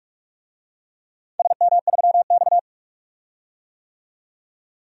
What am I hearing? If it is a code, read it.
Morse code, heard as a keyed tone, about 44 wpm, 700 Hz. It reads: SM3X
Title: DXCC-100 Call Signs - 4 Characters - 45wpm